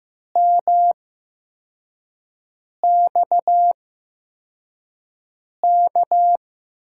Morse code, heard as a keyed tone, about 15 wpm, spaced as for 4 wpm, 700 Hz. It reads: MXK
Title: USA Call Signs - 5 Characters - 8x Character Spacing - 15wpm